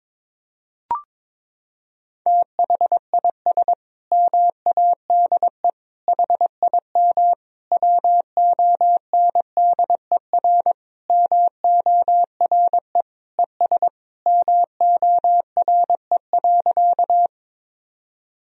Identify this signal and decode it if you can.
Morse code, heard as a keyed tone, about 22 wpm, 700 Hz. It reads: THIS MADE HIM WONDER MORE ES MORE.
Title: Sentences from Top 500 Words and Abbreviations 22wpm